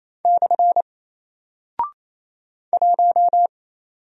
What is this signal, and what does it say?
Morse code, heard as a keyed tone, about 28 wpm, 700 Hz. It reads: / 1
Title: Single Character (Letter:Number:Punctuation:Prosign) Alt 28wpm